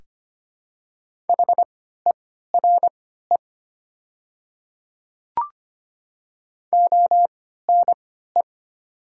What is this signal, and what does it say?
Morse code, heard as a keyed tone, about 25 wpm, spaced as for 14 wpm, 700 Hz. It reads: HERE ONE
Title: 3x Character Spacing - Top 300 Words - 25wpm